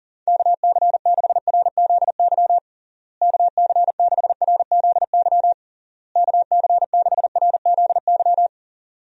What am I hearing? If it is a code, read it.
Morse code, heard as a keyed tone, about 40 wpm, 700 Hz. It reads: KC6RZY KC6RZY KC6RZY
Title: USA Call Signs - 6 Characters - Repeated 3x - 40wpm